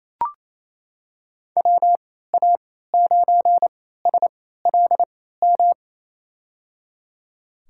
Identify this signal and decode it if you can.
Morse code, heard as a keyed tone, about 27 wpm, spaced as for 16 wpm, 700 Hz. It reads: WA9SLM
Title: USA Call Signs - 6 Characters - 3x Character Spacing - 28wpm